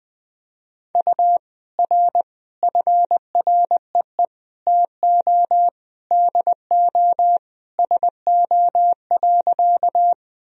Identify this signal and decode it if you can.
Morse code, heard as a keyed tone, about 21 wpm, 700 Hz. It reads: U R FREE TO DO SO.
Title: Sentences from Top 500 Words and Abbreviations 20wpm